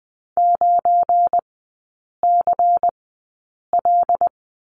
Morse code, heard as a keyed tone, about 20 wpm, spaced as for 8 wpm, 700 Hz. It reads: 9CL
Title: 172 Lesson 28e - Number 9 - Three Characters 20wpm